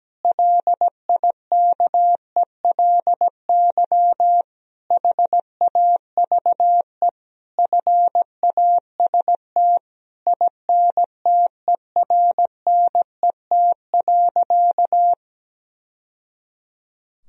Morse code, heard as a keyed tone, about 17 wpm, 700 Hz. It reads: LIKELY HAVE FAST INTERNET.
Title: Sentences from Top 1000 Words - Review Part 4 - 17wpm